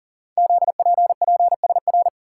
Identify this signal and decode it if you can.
Morse code, heard as a keyed tone, about 42 wpm, 700 Hz. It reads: ZPPSR
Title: Code Groups - 5 Letters - 40wpm